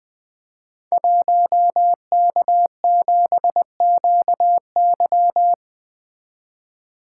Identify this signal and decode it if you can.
Morse code, heard as a keyed tone, about 20 wpm, 700 Hz. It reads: 1K7QY
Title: Code Groups - 5 Letter:Numbers - 20wpm